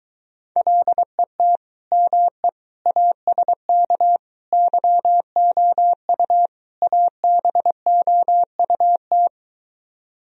Morse code, heard as a keyed tone, about 23 wpm, 700 Hz. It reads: LET ME ASK YOU ABOUT